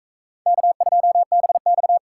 Morse code, heard as a keyed tone, about 44 wpm, 700 Hz. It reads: K2BX